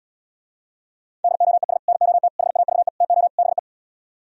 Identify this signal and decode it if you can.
Morse code, heard as a keyed tone, about 77 wpm, 700 Hz. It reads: DOES NOT HAVE TO BE